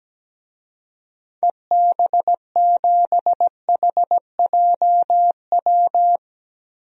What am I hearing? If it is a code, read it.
Morse code, heard as a keyed tone, about 17 wpm, 700 Hz. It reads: EB7HJW
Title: DXCC-100 Call Signs - 6 Characters - Repeated 2x - 17wpm